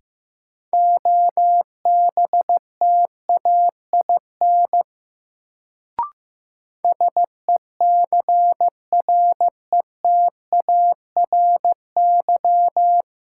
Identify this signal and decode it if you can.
Morse code, heard as a keyed tone, about 16 wpm, 700 Hz. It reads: OBTAIN SECRETARY
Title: Sentences from Top 1300 Words - New Word Drill Alt - 15wpm